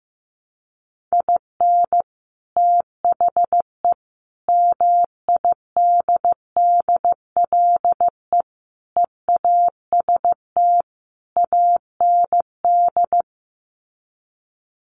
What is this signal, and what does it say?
Morse code, heard as a keyed tone, about 15 wpm, 700 Hz. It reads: IN THE MIDDLE EAST AND
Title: Sets of 5 Words Encore 15wpm